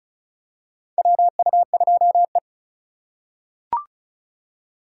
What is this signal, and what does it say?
Morse code, heard as a keyed tone, about 36 wpm, 700 Hz. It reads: WU2E
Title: USA Call Signs - 4 Characters - 3x Character Spacing - 35wpm